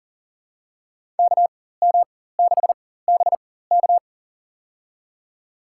Morse code, heard as a keyed tone, about 40 wpm, spaced as for 19 wpm, 700 Hz. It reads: KM6BK
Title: USA Call Signs - 5 Characters - 4x Character Spacing - 40wpm